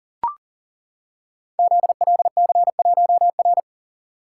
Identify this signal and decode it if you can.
Morse code, heard as a keyed tone, about 37 wpm, 700 Hz. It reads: ZLC1R
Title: Code Groups - 5 Letter:Numbers - 40wpm